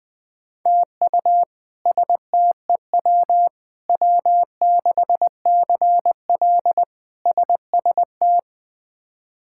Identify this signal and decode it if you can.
Morse code, heard as a keyed tone, about 20 wpm, 700 Hz. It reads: TU STEW W6CL SST
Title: K1USN SST QSO Practice - 20wpm